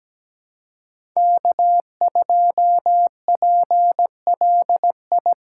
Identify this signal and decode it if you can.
Morse code, heard as a keyed tone, about 17 wpm, 700 Hz. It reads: K2PLI
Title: USA Call Signs - 3x Character Spacing - 17wpm